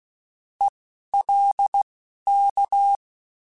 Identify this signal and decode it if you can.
Morse code, heard as a keyed tone, about 16 wpm, spaced as for 11 wpm, 800 Hz. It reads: ELK